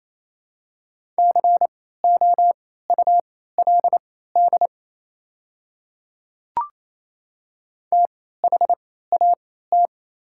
Morse code, heard as a keyed tone, about 28 wpm, spaced as for 16 wpm, 700 Hz. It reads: COULD THAT
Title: 3x Character Spacing - Top 100 Words - 28wpm